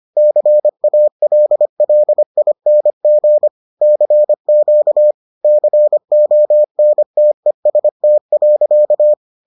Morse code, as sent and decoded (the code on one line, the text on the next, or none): -.-. .- .-.. .-.. .. -. --. / -.-. --.- / -.-. --- -. - . ... - .-.-.-
CALLING CQ CONTEST.